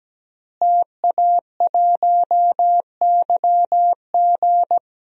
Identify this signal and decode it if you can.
Morse code, heard as a keyed tone, about 17 wpm, 700 Hz. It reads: TA1YG
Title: DXCC-100 Call Signs - 5 Characters - Repeated 3x - 17wpm